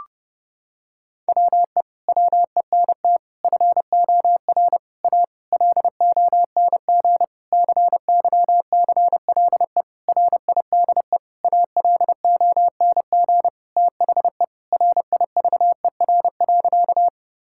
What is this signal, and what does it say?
Morse code, heard as a keyed tone, about 32 wpm, 700 Hz. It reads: WE WENT FOR A LONG CYCLE RIDE ALONG THE RIVER.
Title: Sentences from Top 1000 Words - Review Part 6 - 30wpm